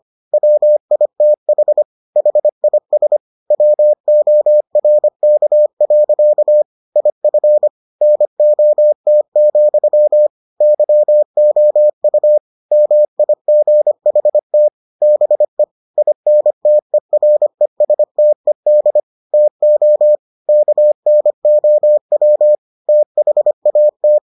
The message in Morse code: WITH HIS WORK. IF NOT, YOU MIGHT BE INTERESTED TO KNOW THAT

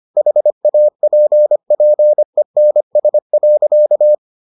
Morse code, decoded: HAPPENS.

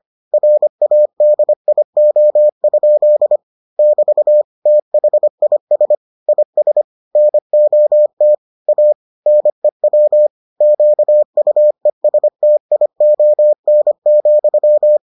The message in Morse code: RADIO? <BT> THIS IS NOT A NEW QUESTION,

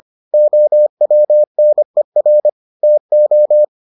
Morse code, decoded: OWNER TO

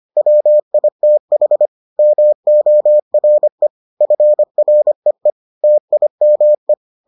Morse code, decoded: WITH MORE FREE TIME

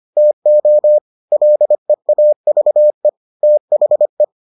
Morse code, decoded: TO LEAVE THE

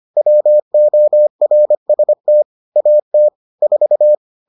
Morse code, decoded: WORST AT 4

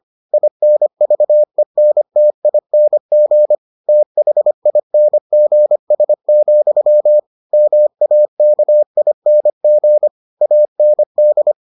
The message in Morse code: INVENTING THINGS, MAKING AND

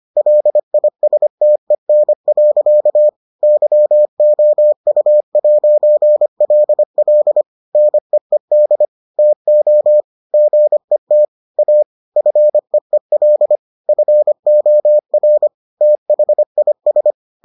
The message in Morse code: LISTEN. YOU'LL NEED TO GET A FEEL FOR THIS

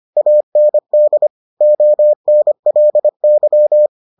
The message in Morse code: AND ONLY

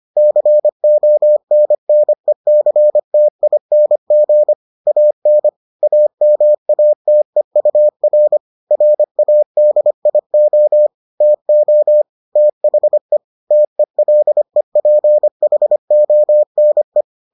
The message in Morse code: CONNECTING AN AMATEUR RADIO TO THE TELEPHONE

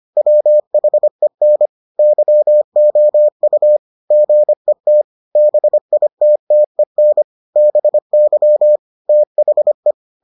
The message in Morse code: WHEN YOU GET BITTEN BY THE

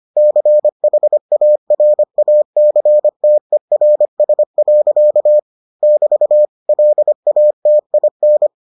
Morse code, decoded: CHARACTERS. <BT> LATIN

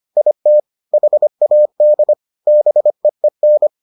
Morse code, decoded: IT HAD BEEN

